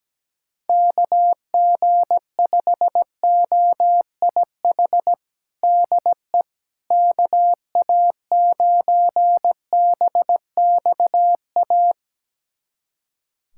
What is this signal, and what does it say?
Morse code, heard as a keyed tone, about 17 wpm, 700 Hz. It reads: KG5OIH DE KA9BXA
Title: USA Callsign DE Callsign - 6 Characters - 17wpm